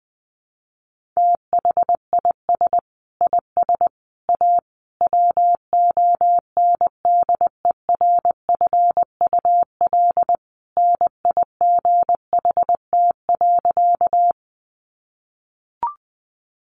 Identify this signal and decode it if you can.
Morse code, heard as a keyed tone, about 20 wpm, 700 Hz. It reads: THIS IS A WONDERFUL NIGHT.
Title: Sentences from Top 1100 Words 20wpm